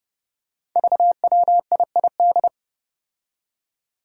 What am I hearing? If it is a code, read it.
Morse code, heard as a keyed tone, about 30 wpm, 700 Hz. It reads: VWIID
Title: Five Challenging Letter-Number Combinations 30wpm